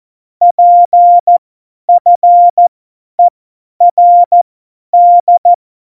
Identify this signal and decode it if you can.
Morse code, heard as a keyed tone, about 14 wpm, spaced as for 10 wpm, 700 Hz. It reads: PFERD